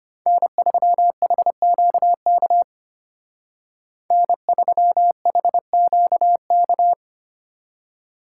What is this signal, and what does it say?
Morse code, heard as a keyed tone, about 32 wpm, 700 Hz. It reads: N3HQK N3HQK